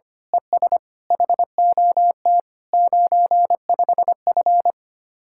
Morse code, heard as a keyed tone, about 24 wpm, 700 Hz. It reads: ES HOT 95F
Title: Speed-Racing Common QSO Phrases Encore - 25wpm